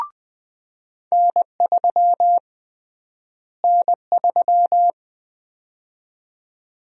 Morse code, heard as a keyed tone, about 21 wpm, 700 Hz. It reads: N3 N3
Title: USA Call Sign Prefixes - Repeated 2x - 20wpm